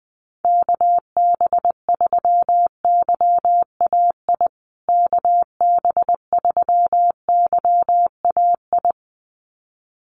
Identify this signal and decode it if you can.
Morse code, heard as a keyed tone, about 20 wpm, 700 Hz. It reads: KB3YAI KB3YAI
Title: USA Call Signs Repeated 2x 20wpm